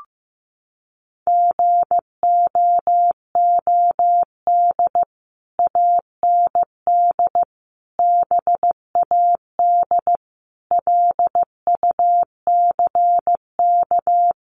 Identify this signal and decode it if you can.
Morse code, heard as a keyed tone, about 15 wpm, 700 Hz. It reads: GOOD AND BAD LUCK
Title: Sentences from Top 1500 Words 15wpm